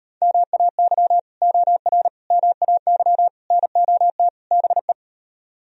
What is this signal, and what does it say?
Morse code, heard as a keyed tone, about 33 wpm, 700 Hz. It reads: MAY OR MAY NOT BE